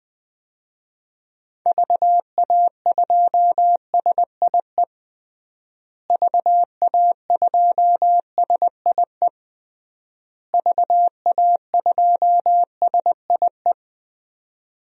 Morse code, heard as a keyed tone, about 20 wpm, 700 Hz. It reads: VA2SIE VA2SIE VA2SIE
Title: DXCC-100 Call Signs - 6 Characters - Repeated 3x - 20wpm